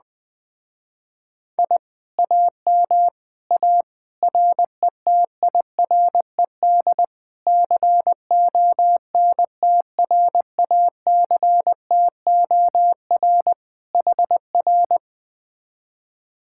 Morse code, heard as a keyed tone, about 20 wpm, 700 Hz. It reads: I AM A RETIRED CONTRACTOR HR